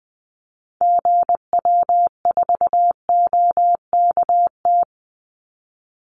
Morse code, heard as a keyed tone, about 20 wpm, 700 Hz. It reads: GW4OKT
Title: Call Signs Repeated 2x - 20wpm